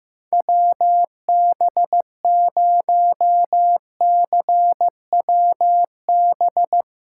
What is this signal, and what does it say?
Morse code, heard as a keyed tone, about 15 wpm, 700 Hz. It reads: WB0CWB
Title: USA Callsign DE Callsign 15wpm